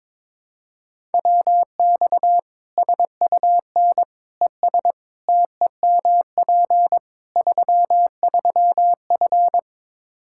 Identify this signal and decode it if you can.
Morse code, heard as a keyed tone, about 22 wpm, 700 Hz. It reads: WX SUN ES TEMP 33F
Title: K1USN SST QSO Practice 22wpm